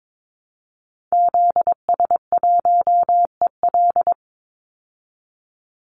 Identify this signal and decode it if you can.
Morse code, heard as a keyed tone, about 22 wpm, 700 Hz. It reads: ZS1EL
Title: Call Signs Repeated 2x - 22wpm